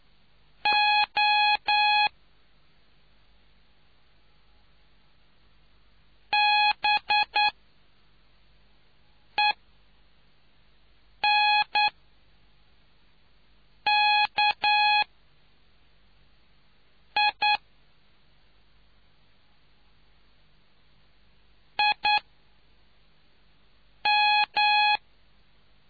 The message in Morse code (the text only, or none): O B E N K I I M